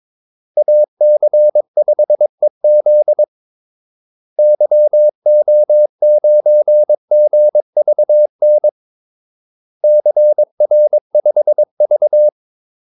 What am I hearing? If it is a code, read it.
Morse code, heard as a keyed tone, about 22 wpm, 600 Hz. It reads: AC5EZ YO9GVN CR5V
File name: Callsigns-22wpm